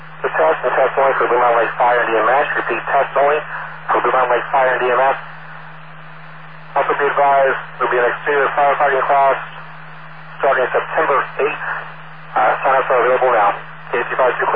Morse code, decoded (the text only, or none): none